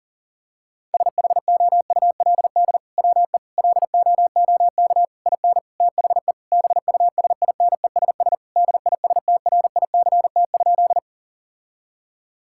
Morse code, heard as a keyed tone, about 37 wpm, 700 Hz. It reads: SHOULD WE LOOK IN THE BUSINESS DISTRICT?
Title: Sentences from Top 1600 Words 40wpm